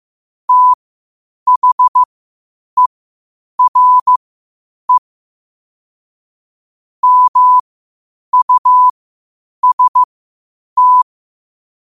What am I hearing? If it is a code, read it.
Morse code, heard as a keyed tone, about 15 wpm, spaced as for 8 wpm, 1000 Hz. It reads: THERE MUST
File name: qotdfiles / 8wpm